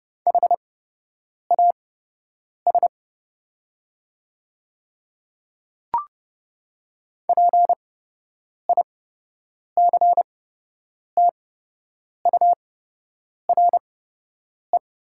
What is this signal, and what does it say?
Morse code, heard as a keyed tone, about 30 wpm, spaced as for 8 wpm, 690 Hz. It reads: HAS PICTURE
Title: 8x Character Spacing - Top 300 Words - 30wpm